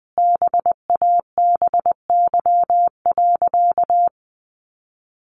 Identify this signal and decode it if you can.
Morse code, heard as a keyed tone, about 20 wpm, 700 Hz. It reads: BABY.
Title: Sentences from Top 1300 Words 20wpm